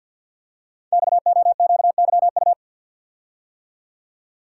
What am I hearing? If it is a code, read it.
Morse code, heard as a keyed tone, about 51 wpm, 700 Hz. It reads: XOQYU
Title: Code Groups - 5 Letters - 50wpm